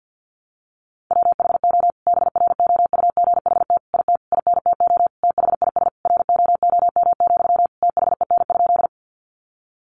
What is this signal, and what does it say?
Morse code, heard as a keyed tone, about 48 wpm, 700 Hz. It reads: WHO BROUGHT IT INTO THIS ROOM, THEN?
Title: Sentences from Top 600 Words 50wpm